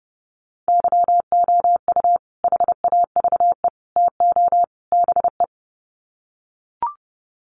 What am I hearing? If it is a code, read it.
Morse code, heard as a keyed tone, about 30 wpm, 700 Hz. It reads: YOU HAVE TO BE
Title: Sets of 4 Words 30wpm